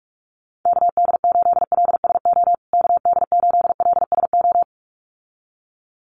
Morse code, heard as a keyed tone, about 44 wpm, 700 Hz. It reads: KD8LSO KD8LSO
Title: USA Call Signs Repeated 2x 45wpm